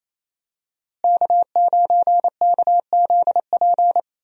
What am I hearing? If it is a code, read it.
Morse code, heard as a keyed tone, about 27 wpm, 700 Hz. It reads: K9KZP
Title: USA Call Signs - 5 Characters - Repeated 3x - 28wpm